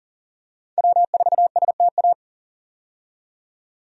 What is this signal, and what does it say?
Morse code, heard as a keyed tone, about 38 wpm, 700 Hz. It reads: W4STA
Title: USA Call Signs - 5 Characters - 40wpm